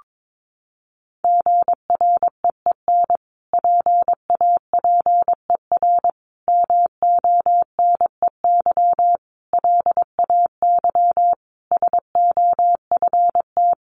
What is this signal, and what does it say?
Morse code, heard as a keyed tone, about 22 wpm, 700 Hz. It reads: GREEN PAPER MONEY LAY SOFT
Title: Sentences from Top 1500 Words 22wpm